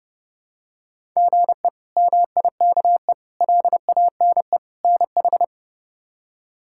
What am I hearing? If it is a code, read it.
Morse code, heard as a keyed tone, about 29 wpm, 700 Hz. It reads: GE MIKE LANE NH
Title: K1USN SST QSO Practice - 30wpm